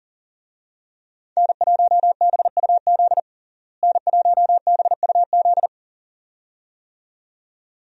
Morse code, heard as a keyed tone, about 40 wpm, 700 Hz. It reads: N1BUZ N1BUZ
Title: USA Call Signs - 5 Characters - Repeated 2x - 40wpm